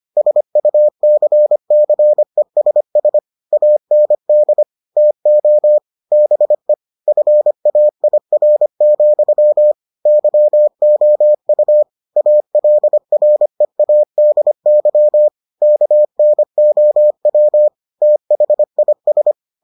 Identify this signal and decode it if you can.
Morse code, heard as a keyed tone, about 25 wpm, 600 Hz. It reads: SUCCESS AND TO BE FAIR, YOU ALREADY KNOW THIS